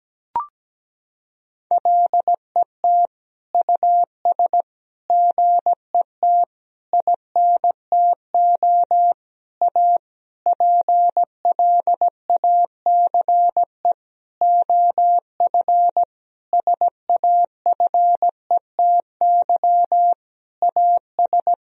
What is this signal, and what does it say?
Morse code, heard as a keyed tone, about 17 wpm, 700 Hz. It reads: LET US GET INTO A PLACE OF SAFETY AS
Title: Sentences from Top 1000 Words - Review Part 4 - 17wpm